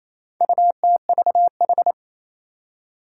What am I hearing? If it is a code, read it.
Morse code, heard as a keyed tone, about 28 wpm, 700 Hz. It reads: UTVH
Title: Four Challenging Letter-Number Combinations 28wpm